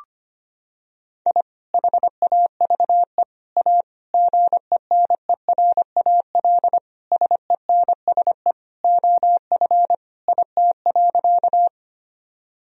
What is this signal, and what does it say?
Morse code, heard as a keyed tone, about 25 wpm, 700 Hz. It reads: I HAVE A GENERAL SENSE OF IT.
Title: Sentences from Top 1000 Words - Review Part 4 - 25wpm